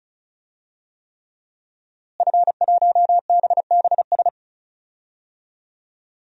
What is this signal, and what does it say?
Morse code, heard as a keyed tone, about 36 wpm, 700 Hz. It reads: F1BBS
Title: DXCC-100 Call Signs - 5 Characters - Repeated 3x - 35wpm